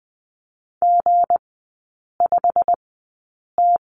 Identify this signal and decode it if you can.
Morse code, heard as a keyed tone, about 20 wpm, spaced as for 8 wpm, 690 Hz. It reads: G5T
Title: 166 Lesson 27e - Number 7 - Three Characters 20wpm